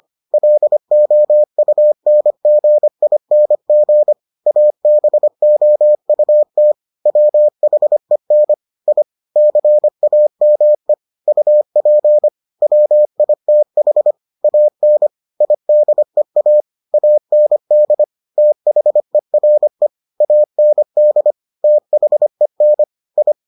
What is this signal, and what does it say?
Morse code, heard as a keyed tone, about 25 wpm, 600 Hz. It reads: LOUNGING ABOUT WHEN I CAME UP WITH AN IDEA AND THERE AND THEN I